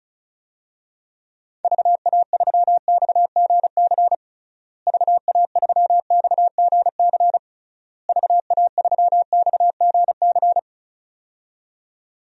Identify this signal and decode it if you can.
Morse code, heard as a keyed tone, about 34 wpm, 700 Hz. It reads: VA3XGC VA3XGC VA3XGC
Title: DXCC-100 Call Signs - 6 Characters - Repeated 3x - 35wpm